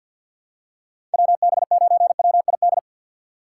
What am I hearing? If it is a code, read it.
Morse code, heard as a keyed tone, about 52 wpm, 700 Hz. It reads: WB9WID
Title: USA Call Signs - 6x Character Spacing - 50wpm